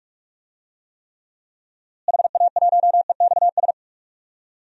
Morse code, heard as a keyed tone, about 43 wpm, 700 Hz. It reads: HA1EXS
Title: DXCC-100 Call Signs - 6 Characters - Repeated 3x - 45wpm